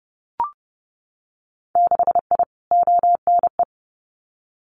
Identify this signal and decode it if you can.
Morse code, heard as a keyed tone, about 30 wpm, 700 Hz. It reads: 6I ONE
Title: Winter Field Day Exchanges Encore 30wpm